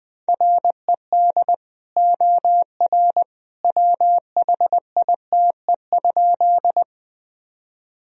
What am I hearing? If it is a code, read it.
Morse code, heard as a keyed tone, about 21 wpm, 700 Hz. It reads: RED OR WHITE?
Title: Sentences from Top 1000 Words - Review Part 2 - 20wpm